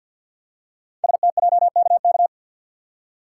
Morse code, heard as a keyed tone, about 52 wpm, 700 Hz. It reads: STJKK
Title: Code Groups - 5 Letters - 50wpm